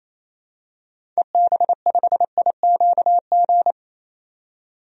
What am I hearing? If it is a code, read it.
Morse code, heard as a keyed tone, about 28 wpm, 700 Hz. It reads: EB5IQG
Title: DXCC-100 Call Signs - 6 Characters - 28wpm